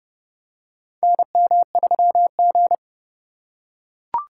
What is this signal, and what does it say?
Morse code, heard as a keyed tone, about 30 wpm, 700 Hz. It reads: NM3G
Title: USA Call Signs - 4 Characters - 4x Character Spacing - 30wpm